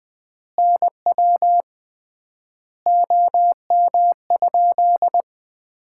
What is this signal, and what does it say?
Morse code, heard as a keyed tone, about 20 wpm, 700 Hz. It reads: NW OM?
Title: Common QSO Phrases - 3x Word-Spacing - 20wpm